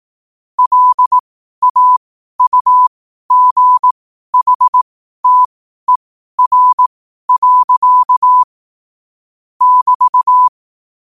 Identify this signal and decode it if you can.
Morse code, heard as a keyed tone, about 18 wpm, spaced as for 12 wpm, 1000 Hz. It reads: LAUGHTER. <BT>
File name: qotdfiles / 12wpm